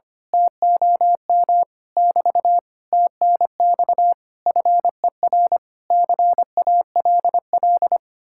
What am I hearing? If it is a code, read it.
Morse code, heard as a keyed tone, about 24 wpm, 700 Hz. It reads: TOM <BT> TNX FER CALL
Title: Speed-Racing Common QSO Phrases - 25wpm